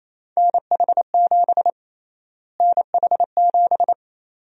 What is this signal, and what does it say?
Morse code, heard as a keyed tone, about 28 wpm, 700 Hz. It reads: NH7 NH7
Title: USA Call Sign Prefixes - Repeated 2x - 28wpm